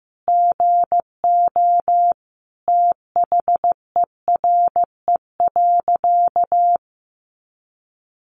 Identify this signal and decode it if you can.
Morse code, heard as a keyed tone, about 15 wpm, 700 Hz. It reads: GO THERE.
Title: Sentences from Top 800 Words 15wpm